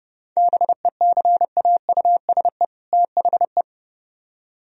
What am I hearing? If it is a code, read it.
Morse code, heard as a keyed tone, about 36 wpm, 700 Hz. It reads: BECAUSE THE
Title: Speed-Racing - Sets of 2 Words 30wpm